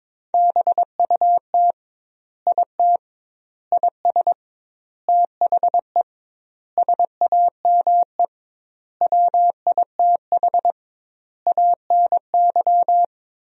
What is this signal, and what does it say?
Morse code, heard as a keyed tone, about 22 wpm, 700 Hz. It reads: BUT IT IS THE SAME WITH ANY
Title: 2x Word-Spacing - Sentences from Top 300 Words - 22wpm